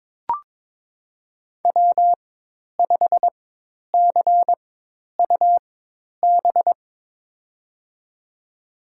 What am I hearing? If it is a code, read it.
Morse code, heard as a keyed tone, about 21 wpm, spaced as for 10 wpm, 700 Hz. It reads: W5CUB
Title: USA Call Signs - 5 Characters - 4x Character Spacing - 22wpm